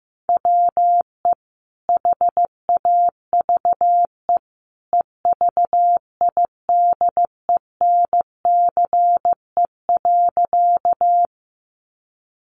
Morse code, heard as a keyed tone, about 15 wpm, 700 Hz. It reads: WE HAVE EVIDENCE.
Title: Sentences from Top 1300 Words 15wpm